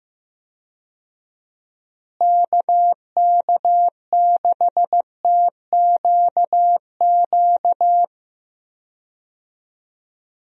Morse code, heard as a keyed tone, about 15 wpm, 700 Hz. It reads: KK6TQQ
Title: USA Call Signs - 6 Characters - Repeated 3x - 15wpm